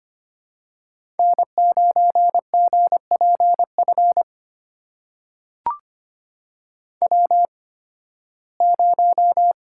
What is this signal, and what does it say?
Morse code, heard as a keyed tone, about 25 wpm, spaced as for 16 wpm, 700 Hz. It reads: N9GPF W0
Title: USA Call Signs - 5 Characters - 8x Character Spacing - 25wpm